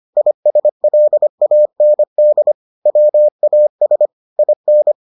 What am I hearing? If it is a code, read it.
Morse code, heard as a keyed tone, about 25 wpm, 600 Hz. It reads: ISLAND WAS IN